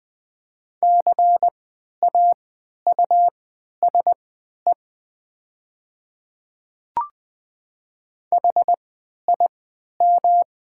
Morse code, heard as a keyed tone, about 20 wpm, spaced as for 11 wpm, 700 Hz. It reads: CAUSE HIM
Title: 3x Character Spacing - Top 200 Words - 20wpm